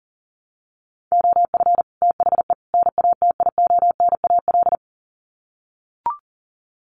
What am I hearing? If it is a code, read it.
Morse code, heard as a keyed tone, about 40 wpm, 700 Hz. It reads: OF THE NATIONAL
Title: Sets of 3 Words 40wpm